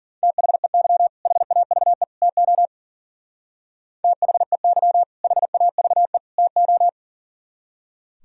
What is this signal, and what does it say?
Morse code, heard as a keyed tone, about 52 wpm, 700 Hz. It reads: THEY HAVE TO THEY HAVE TO